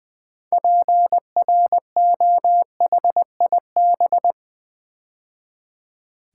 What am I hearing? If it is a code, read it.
Morse code, heard as a keyed tone, about 20 wpm, 700 Hz. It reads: PROHIB
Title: Build A Word - 3 Letters At A Time - 8-Letter Words - 20wpm